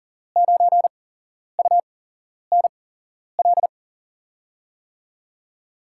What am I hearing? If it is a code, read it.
Morse code, heard as a keyed tone, about 40 wpm, spaced as for 11 wpm, 700 Hz. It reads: 9UNL
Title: USA Call Signs - 6 Characters - 8x Character Spacing - 40wpm